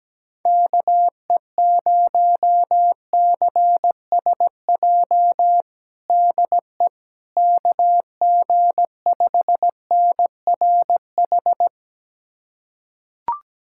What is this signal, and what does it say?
Morse code, heard as a keyed tone, about 17 wpm, 700 Hz. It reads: KE0CSJ DE KG5NRH
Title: USA Callsign DE Callsign - 6 Characters - 17wpm